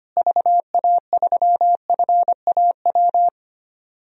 Morse code, heard as a keyed tone, about 25 wpm, 700 Hz. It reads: VA3FAW
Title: DXCC-100 Call Signs - 6 Characters - Repeated 3x - 25wpm